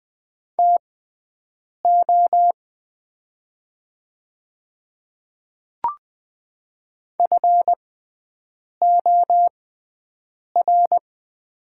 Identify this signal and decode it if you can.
Morse code, heard as a keyed tone, about 20 wpm, spaced as for 7 wpm, 700 Hz. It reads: TO FOR
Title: 6x Character Spacing - Top 200 Words - 20wpm